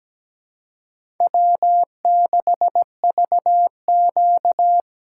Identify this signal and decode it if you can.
Morse code, heard as a keyed tone, about 17 wpm, 700 Hz. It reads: W6VQ
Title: USA Call Signs - 4x Character Spacing - 17wpm